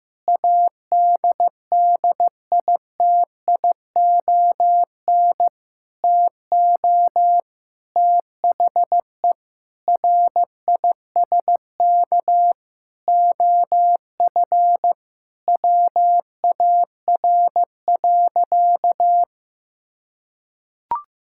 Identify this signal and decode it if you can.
Morse code, heard as a keyed tone, about 15 wpm, 700 Hz. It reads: ADDITION TO THE RISK OF WAR.
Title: Sentences from Top 1000 Words - Review Part 6 - 15wpm